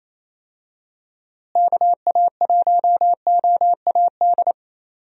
Morse code, heard as a keyed tone, about 28 wpm, 700 Hz. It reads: KA1OAD
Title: USA Call Signs - 6 Characters - Repeated 3x - 28wpm